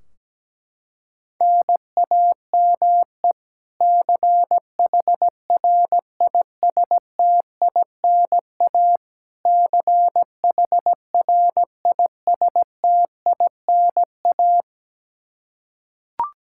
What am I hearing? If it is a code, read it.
Morse code, heard as a keyed tone, about 17 wpm, 700 Hz. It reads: NAME CHRISTINA CHRISTINA
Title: K1USN SST QSO Practice 17wpm